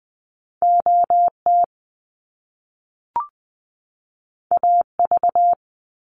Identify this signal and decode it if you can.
Morse code, heard as a keyed tone, about 20 wpm, 700 Hz. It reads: OT AV
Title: Top Two Letter Combinations in Words 20wpm